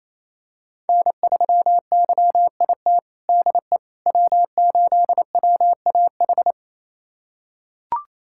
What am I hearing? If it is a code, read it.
Morse code, heard as a keyed tone, about 28 wpm, 700 Hz. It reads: N3YIT DE W8WAH
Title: USA Callsign DE Callsign - 5 Characters - 28wpm